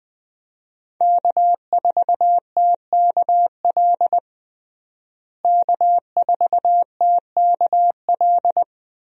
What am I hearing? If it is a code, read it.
Morse code, heard as a keyed tone, about 20 wpm, 700 Hz. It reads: K4TKL K4TKL
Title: USA Call Signs - 5 Characters - Repeated 2x - 20wpm